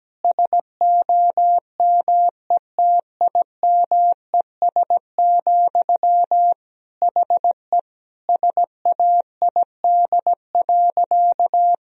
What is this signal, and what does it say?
Morse code, heard as a keyed tone, about 17 wpm, 700 Hz. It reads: SOMETIMES, HE SAID.
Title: Sentences from Top 1000 Words - Review Part 3 - 17wpm